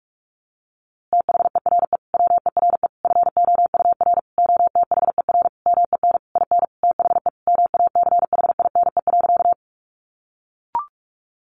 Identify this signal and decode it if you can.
Morse code, heard as a keyed tone, about 44 wpm, 700 Hz. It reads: THERE WERE FOUR OTHER MEN IN THE MACHINE.
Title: A Sound Of Thunder 45wpm